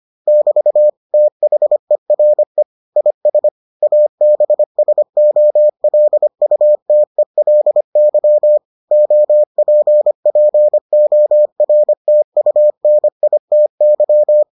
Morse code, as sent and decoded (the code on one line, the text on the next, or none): -...- / - .... .-. / .. ... / .- -... ... --- .-.. ..- - . .-.. -.-- / --- .--. .--. --- .-. - ..- -. .. - -.--
<BT> THERE IS ABSOLUTELY OPPORTUNITY